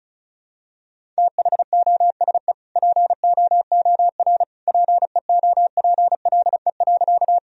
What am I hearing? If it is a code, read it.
Morse code, heard as a keyed tone, about 36 wpm, 700 Hz. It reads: THOSE POOR PEOPLE.
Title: Sentences from Top 1000 Words - Review Part 1 - 35wpm